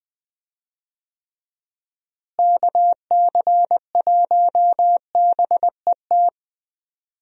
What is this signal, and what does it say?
Morse code, heard as a keyed tone, about 20 wpm, 700 Hz. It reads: KC1BET
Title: USA Call Signs - 6 Characters - Repeated 2x - 20wpm